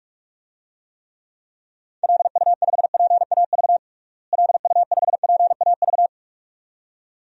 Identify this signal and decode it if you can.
Morse code, heard as a keyed tone, about 41 wpm, 700 Hz. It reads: LU5PAV LU5PAV
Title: DXCC-100 Call Signs - 6 Characters - Repeated 2x - 45wpm